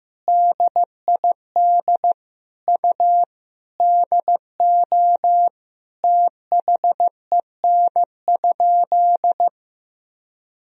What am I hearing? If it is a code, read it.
Morse code, heard as a keyed tone, about 15 wpm, 700 Hz. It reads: DID U DO THEN?
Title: Sentences from Top 100 Words and Abbreviations 15wpm